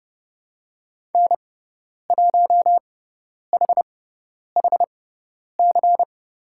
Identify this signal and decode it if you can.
Morse code, heard as a keyed tone, about 30 wpm, spaced as for 10 wpm, 700 Hz. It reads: N1HHC
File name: Lesson 18g - Letter m - Call Signs 30wpm